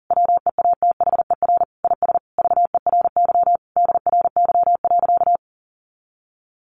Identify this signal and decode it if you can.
Morse code, heard as a keyed tone, about 40 wpm, 700 Hz. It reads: WEATHER IS VERY DRY.
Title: Sentences from Top 1400 Words 40wpm